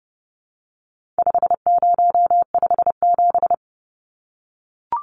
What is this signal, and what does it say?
Morse code, heard as a keyed tone, about 30 wpm, 700 Hz. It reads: 5057
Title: Numbers - 4-Digit 30wpm